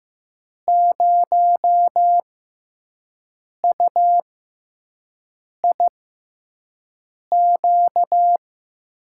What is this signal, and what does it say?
Morse code, heard as a keyed tone, about 15 wpm, spaced as for 5 wpm, 700 Hz. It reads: 0UIQ
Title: USA Call Signs - 5 Characters - 6x Character Spacing - 15wpm